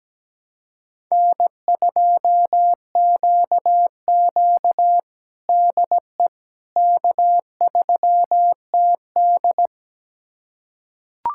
Morse code, heard as a keyed tone, about 17 wpm, 700 Hz. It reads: N2QQ DE K3TD
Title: USA Callsign DE Callsign - 4 Characters - 17wpm